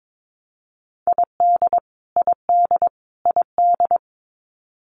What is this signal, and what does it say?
Morse code, heard as a keyed tone, about 22 wpm, 700 Hz. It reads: ID ID ID
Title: US States Repeated 3x 22wpm